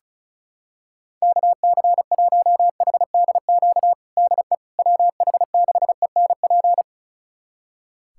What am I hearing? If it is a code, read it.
Morse code, heard as a keyed tone, about 34 wpm, 700 Hz. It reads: KC1HDQ DE WH6ENP
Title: USA Callsign DE Callsign - 6 Characters - 35wpm